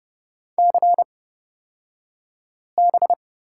Morse code, heard as a keyed tone, about 30 wpm, spaced as for 5 wpm, 700 Hz. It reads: CB
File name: Lesson 36d - BT - Two Characters 30wpm